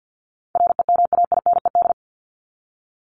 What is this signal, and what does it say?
Morse code, heard as a keyed tone, about 49 wpm, 700 Hz. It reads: REMAINED
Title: ED Suffix - Top 100 Words with Base Words 50wpm